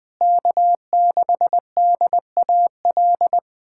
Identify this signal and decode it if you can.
Morse code, heard as a keyed tone, about 21 wpm, 700 Hz. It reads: K6DAL
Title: USA Callsign DE Callsign - 5 Characters - 20wpm